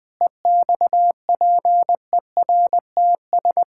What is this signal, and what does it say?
Morse code, heard as a keyed tone, about 20 wpm, 700 Hz. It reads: EXPERTS
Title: Pangram Challenge v2 - 20wpm